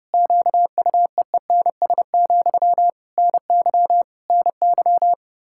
Morse code, heard as a keyed tone, about 30 wpm, 700 Hz. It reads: QUEENS, NY NY
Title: K1USN SST QSO Practice 30wpm